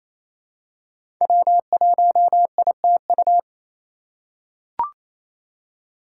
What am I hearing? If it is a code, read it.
Morse code, heard as a keyed tone, about 28 wpm, 700 Hz. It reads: W1ITU
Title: USA Call Signs - 4x Character Spacing - 28wpm